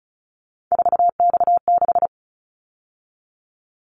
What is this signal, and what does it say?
Morse code, heard as a keyed tone, about 34 wpm, 700 Hz. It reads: 4X6
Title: Callsign Prefixes for Top 50 DXCC - 35wpm